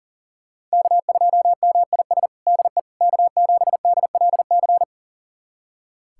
Morse code, heard as a keyed tone, about 40 wpm, 700 Hz. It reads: K2MIS DE K7DLC
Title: USA Callsign DE Callsign - 5 Characters - 40wpm